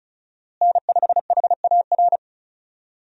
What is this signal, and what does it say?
Morse code, heard as a keyed tone, about 34 wpm, 700 Hz. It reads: N5HAR
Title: USA Call Signs - 5 Characters - Repeated 3x - 35wpm